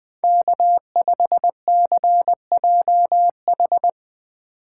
Morse code, heard as a keyed tone, about 20 wpm, 700 Hz. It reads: K5CJH
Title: Speed Racing - USA 1x3 Call Signs 17wpm